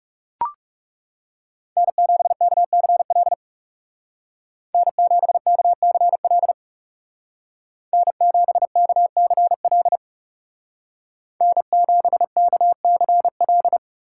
Morse code, heard as a keyed tone, about 35 wpm, 700 Hz. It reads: N7KCL N7KCL N7KCL N7KCL
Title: Speed Racing - USA 1x3 Call Signs 30wpm